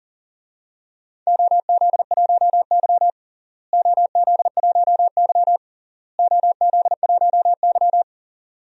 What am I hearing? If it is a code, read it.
Morse code, heard as a keyed tone, about 40 wpm, 700 Hz. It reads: OZ1Y OZ1Y OZ1Y
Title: DXCC-100 Call Signs - 4 Characters - Repeated 3x - 40wpm